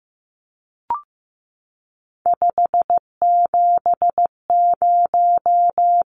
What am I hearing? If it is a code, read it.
Morse code, heard as a keyed tone, about 15 wpm, 700 Hz. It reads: 570
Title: Numbers - 3-Digit 15wpm